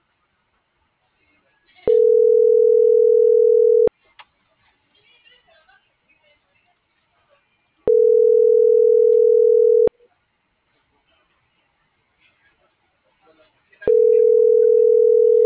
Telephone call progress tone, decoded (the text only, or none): ringback tone